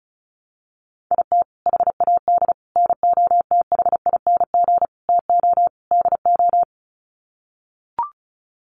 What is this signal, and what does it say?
Morse code, heard as a keyed tone, about 34 wpm, 700 Hz. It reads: IT HAD NOTHING TO DO
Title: Sets of 5 Words Encore2 35wpm